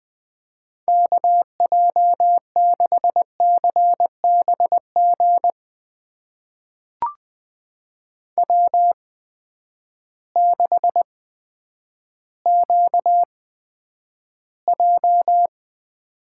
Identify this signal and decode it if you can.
Morse code, heard as a keyed tone, about 20 wpm, spaced as for 10 wpm, 700 Hz. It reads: KJ6CBG W6QJ
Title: USA Call Signs - 8x Character Spacing - 20wpm